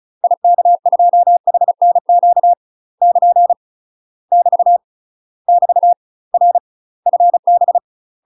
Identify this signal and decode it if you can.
Morse code, heard as a keyed tone, about 36 wpm, 700 Hz. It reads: IK2HNQ <KN> <BT> <BT> R FB
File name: UK MASTER QSOs TRAINING / 35WPM / UK MASTER QSOs 0036